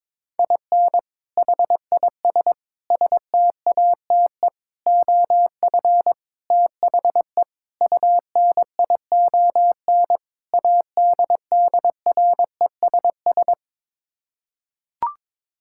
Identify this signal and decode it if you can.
Morse code, heard as a keyed tone, about 22 wpm, 700 Hz. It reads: IN HIS STATE OF THE UNION ADDRESS
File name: practice / 22wpm / Sets of 7 Words - 22wpm